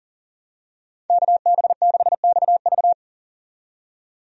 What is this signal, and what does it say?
Morse code, heard as a keyed tone, about 40 wpm, 700 Hz. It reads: KB6XV
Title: USA Call Signs - 5 Characters - 40wpm